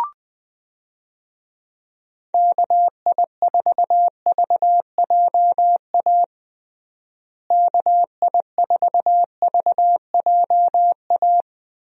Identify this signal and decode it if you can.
Morse code, heard as a keyed tone, about 20 wpm, 700 Hz. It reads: KI4VJA KI4VJA
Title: USA Call Signs - 6 Characters - Repeated 2x - 20wpm